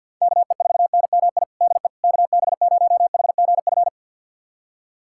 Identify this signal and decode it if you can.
Morse code, heard as a keyed tone, about 48 wpm, 700 Hz. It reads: KE4NMI DE KB0HGF
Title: USA Callsign DE Callsign - 6 Characters - 50wpm